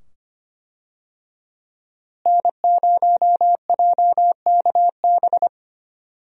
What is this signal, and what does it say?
Morse code, heard as a keyed tone, about 25 wpm, 700 Hz. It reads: N0JKB
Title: USA Call Signs - 5 Characters - Repeated 2x - 25wpm